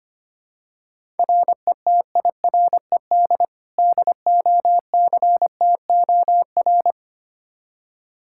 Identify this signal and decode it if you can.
Morse code, heard as a keyed tone, about 25 wpm, 700 Hz. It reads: RETIRED DOCTOR
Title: Practice QSO - Exchange 3 - More Info - 25wpm